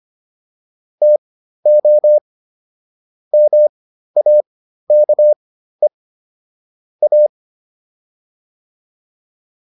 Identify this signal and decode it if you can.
Morse code, heard as a keyed tone, about 25 wpm, spaced as for 13 wpm, 600 Hz. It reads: TO MAKE A